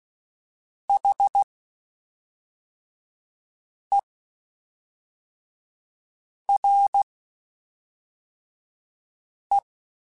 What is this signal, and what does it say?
Morse code, heard as a keyed tone, about 16 wpm, spaced as for 3 wpm, 800 Hz. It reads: HERE
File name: (spaced) / QSOs Spaced-mp3s / kc7sac @5 sp